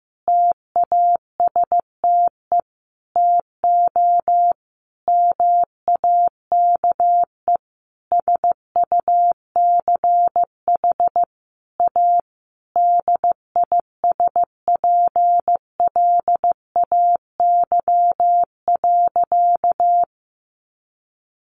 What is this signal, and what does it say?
Morse code, heard as a keyed tone, about 15 wpm, 700 Hz. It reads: TASTE TO MAKE SUCH A DISPLAY.
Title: Sentences from Top 1200 Words 15wpm